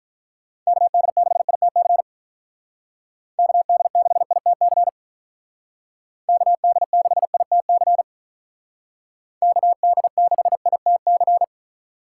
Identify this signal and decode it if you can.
Morse code, heard as a keyed tone, about 45 wpm, 700 Hz. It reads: KD6ITC KD6ITC KD6ITC KD6ITC